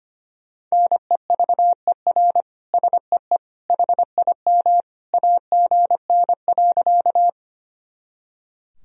Morse code, heard as a keyed tone, about 25 wpm, 700 Hz. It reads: NEVER SEE HIM AGN.
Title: Sentences from Top 300 Words and Abbreviations 25wpm